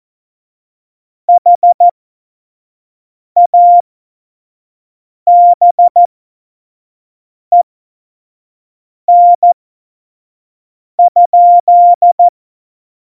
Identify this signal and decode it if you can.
Morse code, heard as a keyed tone, about 14 wpm, spaced as for 5 wpm, 700 Hz. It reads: HABEN?
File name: Morsekurs / Witze / Witz 05. Januar 2021, 25 F70